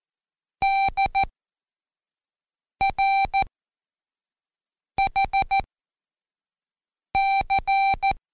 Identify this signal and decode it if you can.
Morse code, heard as a keyed tone, about 14 wpm, spaced as for 5 wpm, 780 Hz. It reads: DRHC
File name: Code Quick / AudioDownld1 / Seg37-Day4-Quiz CHLN